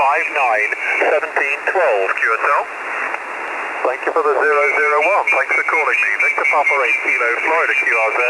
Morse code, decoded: none